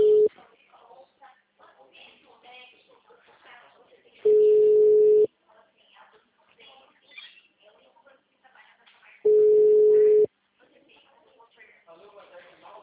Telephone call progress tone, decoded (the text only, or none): ringback tone